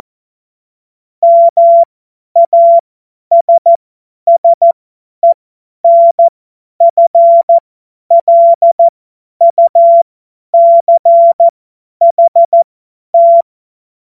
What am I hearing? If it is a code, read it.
Morse code, heard as a keyed tone, about 14 wpm, spaced as for 10 wpm, 680 Hz. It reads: MASSENFLUCHT
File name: Morsekurs / Witze / Witz 09. Dezember 2025 50 F70